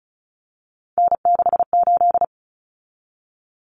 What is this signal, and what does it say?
Morse code, heard as a keyed tone, about 34 wpm, 700 Hz. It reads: N68
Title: Common Cut Numbers - 3-Digit 35wpm